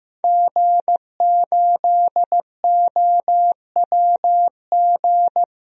Morse code, heard as a keyed tone, about 15 wpm, 700 Hz. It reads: G8OWG
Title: DXCC-100 Call Signs - 5 Characters - Repeated 3x - 15wpm